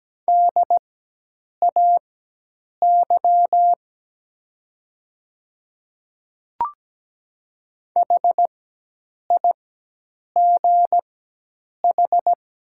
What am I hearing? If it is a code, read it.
Morse code, heard as a keyed tone, about 17 wpm, spaced as for 8 wpm, 700 Hz. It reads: DAY HIGH
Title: 4x Character Spacing - Top 300 Words - 17wpm